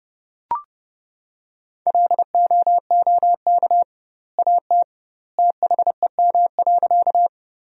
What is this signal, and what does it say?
Morse code, heard as a keyed tone, about 30 wpm, 700 Hz. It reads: LOOK AT THEM.
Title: 2x Word-Spacing - Sentences from Top 100 Words - 30wpm